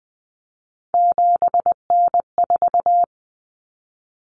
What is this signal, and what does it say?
Morse code, heard as a keyed tone, about 20 wpm, 700 Hz. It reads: 7N4
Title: Common Cut Numbers - 3-Digit 20wpm